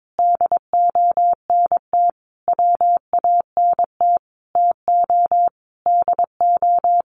Morse code, heard as a keyed tone, about 22 wpm, 700 Hz. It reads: DONT WANT TO DO
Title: Sentences from Top 1000 Words 22wpm